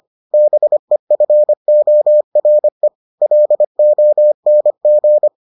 Morse code, heard as a keyed tone, about 25 wpm, 600 Hz. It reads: BEFORE LONG